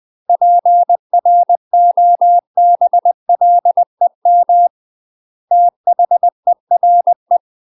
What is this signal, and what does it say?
Morse code, heard as a keyed tone, about 20 wpm, 700 Hz. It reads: PROBLEM THERE
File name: 20WPM